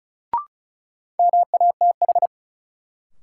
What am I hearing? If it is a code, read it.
Morse code, heard as a keyed tone, about 36 wpm, 700 Hz. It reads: MATH
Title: Sentences from Top 1000 Words - Review Part 6 - New Word Drill - 35wpm